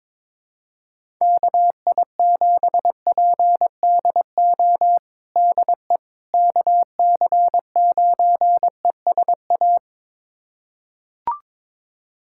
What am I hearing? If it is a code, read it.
Morse code, heard as a keyed tone, about 22 wpm, 700 Hz. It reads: KI7PDO DE KC9ESA
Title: USA Callsign DE Callsign - 6 Characters - 22wpm